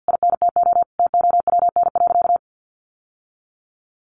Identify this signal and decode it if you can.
Morse code, heard as a keyed tone, about 50 wpm, 700 Hz. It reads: INTO TOWN.
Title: Sentences from Top 1300 Words 50wpm